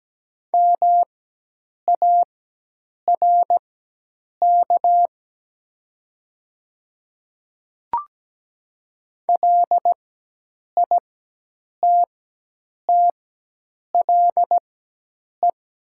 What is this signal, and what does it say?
Morse code, heard as a keyed tone, about 17 wpm, spaced as for 8 wpm, 700 Hz. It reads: MARK LITTLE
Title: 4x Character Spacing - Top 300 Words - 17wpm